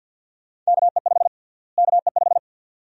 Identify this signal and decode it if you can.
Morse code, heard as a keyed tone, about 51 wpm, 700 Hz. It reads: KE5 KE5
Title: USA Call Sign Prefixes - Repeated 2x - 50wpm